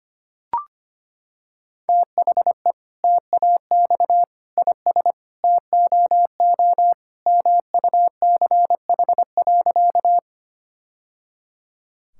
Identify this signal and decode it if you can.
Morse code, heard as a keyed tone, about 25 wpm, 700 Hz. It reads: THE TAX IS TOO MUCH.
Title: Sentences from Top 1000 Words - Review Part 5 - 25wpm